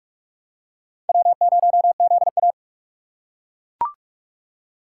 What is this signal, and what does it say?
Morse code, heard as a keyed tone, about 37 wpm, 700 Hz. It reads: W0ZA
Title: Speed Racing - USA 1x2 Call Signs 30wpm